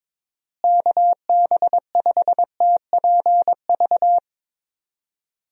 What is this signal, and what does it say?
Morse code, heard as a keyed tone, about 22 wpm, 700 Hz. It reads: KB5TPV
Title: USA Call Signs - 6 Characters - Repeated 3x - 22wpm